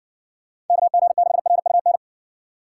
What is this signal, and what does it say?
Morse code, heard as a keyed tone, about 55 wpm, 700 Hz. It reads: KG6RUN